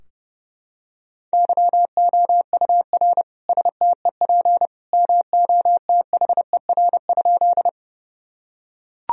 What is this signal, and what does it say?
Morse code, heard as a keyed tone, about 30 wpm, 700 Hz. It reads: YOUR STEP MOTHER?
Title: Sentences from Top 1000 Words - Review Part 2 - 30wpm